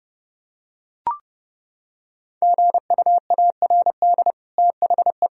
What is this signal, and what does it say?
Morse code, heard as a keyed tone, about 32 wpm, 700 Hz. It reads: GUARD THE
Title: Sentences from Top 1000 Words - Review Part 4 - 30wpm